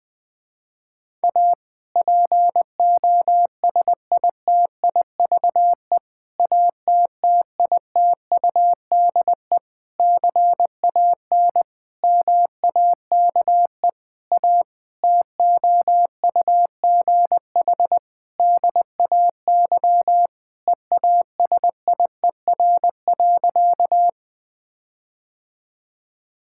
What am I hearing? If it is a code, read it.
Morse code, heard as a keyed tone, about 21 wpm, 700 Hz. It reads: A POSITIVE ATTITUDE CAN MAKE A TOUGH DAY EASIER.
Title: Difficult Sentences with Hints - 20wpm